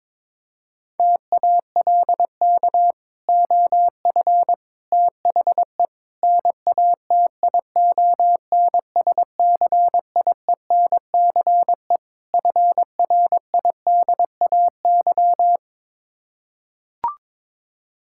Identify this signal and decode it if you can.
Morse code, heard as a keyed tone, about 22 wpm, 700 Hz. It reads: TALK OF THE NATION SCIENCE FRIDAY
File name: practice / 22wpm / Build A Phrase - 5-Word Phrases - 22wpm